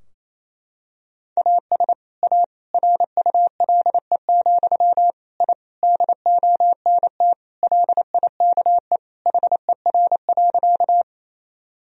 Morse code, heard as a keyed tone, about 28 wpm, 700 Hz. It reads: AS A RULE, I DONT LIKE HER.
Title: Sentences from Top 1000 Words - Review Part 2 - 28wpm